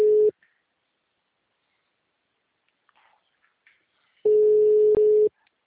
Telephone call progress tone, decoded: ringback tone

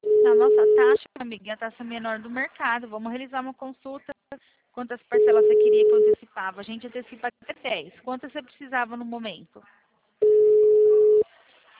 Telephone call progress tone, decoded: ringback tone

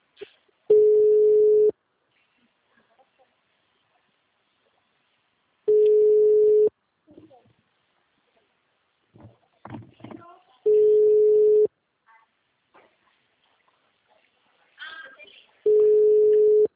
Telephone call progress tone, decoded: ringback tone

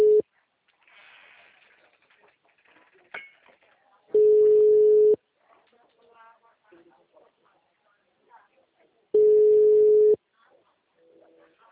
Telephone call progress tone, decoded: ringback tone

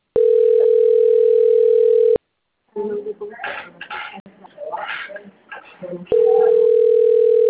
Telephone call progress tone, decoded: ringback tone

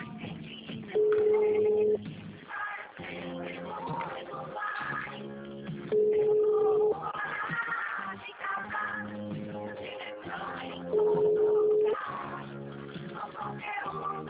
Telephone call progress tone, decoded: ringback tone

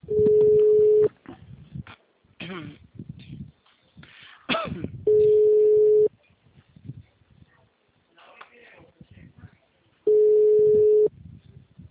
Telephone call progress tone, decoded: ringback tone

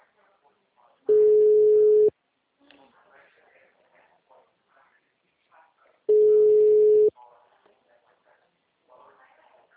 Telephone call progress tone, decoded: ringback tone

